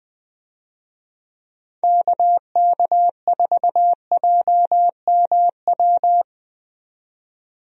Morse code, heard as a keyed tone, about 21 wpm, 700 Hz. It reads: KK4JMW
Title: USA Call Signs - 6 Characters - 20wpm